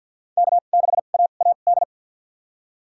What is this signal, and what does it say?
Morse code, heard as a keyed tone, about 52 wpm, spaced as for 36 wpm, 700 Hz. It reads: K6AAD
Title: USA Call Signs - 5 Characters - 2x Character Spacing - 50wpm